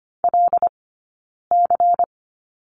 Morse code, heard as a keyed tone, about 25 wpm, 700 Hz. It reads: L C
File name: course / 25:10wpm / 092 Lesson 16d - Letter c - Two Characters 25wpm